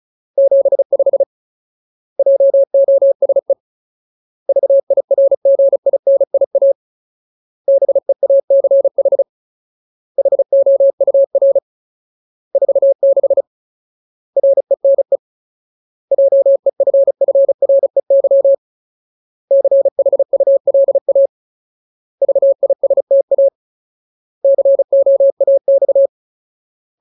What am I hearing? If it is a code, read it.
Morse code, heard as a keyed tone, about 35 wpm, 550 Hz. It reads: 75 JOSE VIRGINIA BEACH HOUR 46 RENE JEFFREY CHULA VISTA COAX
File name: Ultimate Word List / 35WPM Ultimate